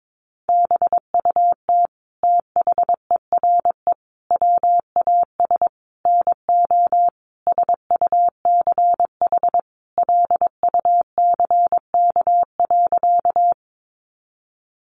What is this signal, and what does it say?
Morse code, heard as a keyed tone, about 22 wpm, 700 Hz. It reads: BUT THERE WAS NO SUCH LUCK.
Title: Sentences from Top 1500 Words 22wpm